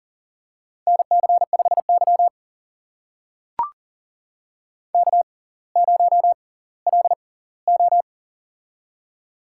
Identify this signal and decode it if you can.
Morse code, heard as a keyed tone, about 39 wpm, spaced as for 20 wpm, 700 Hz. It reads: NC5Y K0LO
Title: USA Call Signs - 4 Characters - 6x Character Spacing - 40wpm